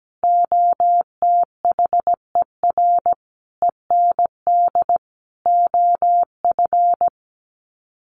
Morse code, heard as a keyed tone, about 17 wpm, 700 Hz. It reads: OTHER END OF
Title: Sets of 5 Words 17wpm